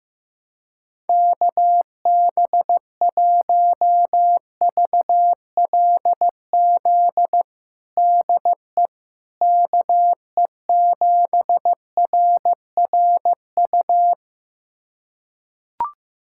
Morse code, heard as a keyed tone, about 15 wpm, 700 Hz. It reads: KB1VLZ DE KE7RRU
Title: USA Callsign DE Callsign 15wpm